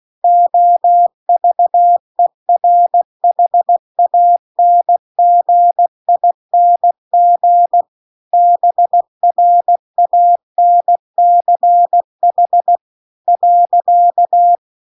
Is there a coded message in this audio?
Yes: Morse code from 0.2 to 14.6 s, heard as a keyed tone, about 16 wpm, 700 Hz. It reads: OVERHANGING BRANCH .